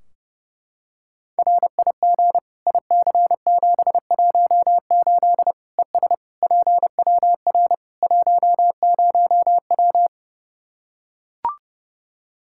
Morse code, heard as a keyed tone, about 30 wpm, 700 Hz. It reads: RIG IC718 ES PWR 10W
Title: Practice QSO - Exchange 2 - Info - 30wpm